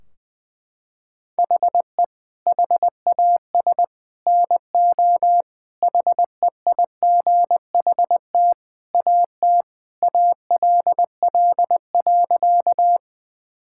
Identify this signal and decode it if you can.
Morse code, heard as a keyed tone, about 20 wpm, 700 Hz. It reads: HE HAS NO HEIGHT AT ALL.
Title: Sentences from Top 1600 Words 20wpm